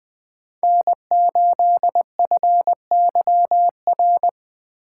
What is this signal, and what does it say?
Morse code, heard as a keyed tone, about 20 wpm, 700 Hz. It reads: N8FYR